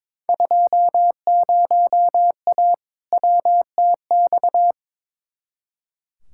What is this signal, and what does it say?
Morse code, heard as a keyed tone, about 22 wpm, 700 Hz. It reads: 20A WTX
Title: Field Day Exchanges Encore - 22wpm